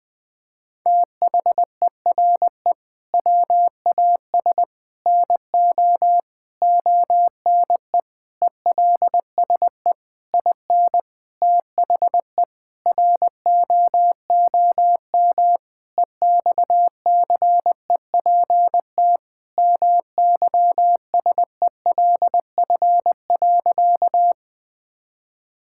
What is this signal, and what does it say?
Morse code, heard as a keyed tone, about 21 wpm, 700 Hz. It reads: THERE WAS NO ONE ELSE IN THE ROOM EXCEPT MYSELF.
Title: Sentences from Top 1600 Words 20wpm